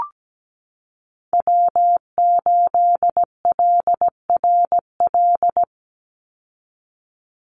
Morse code, heard as a keyed tone, about 17 wpm, 700 Hz. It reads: W8LRL
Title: USA Call Signs 17wpm